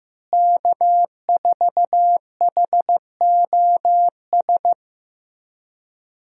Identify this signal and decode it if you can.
Morse code, heard as a keyed tone, about 15 wpm, 700 Hz. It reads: K4HOS
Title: USA Call Signs - 5 Characters - Repeated 3x - 15wpm